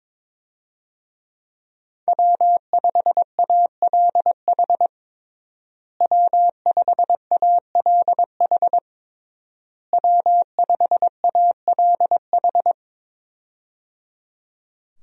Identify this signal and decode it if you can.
Morse code, heard as a keyed tone, about 22 wpm, 700 Hz. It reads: W5ALH W5ALH W5ALH
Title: USA Call Signs - 5 Characters - Repeated 3x - 22wpm